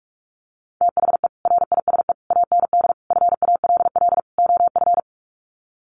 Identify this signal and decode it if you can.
Morse code, heard as a keyed tone, about 44 wpm, 700 Hz. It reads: THE RISE AND FALL OF
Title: Sets of 5 Words Encore2 45wpm